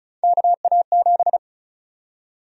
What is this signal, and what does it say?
Morse code, heard as a keyed tone, about 34 wpm, 700 Hz. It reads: KA7
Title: USA Call Sign Prefixes - 35wpm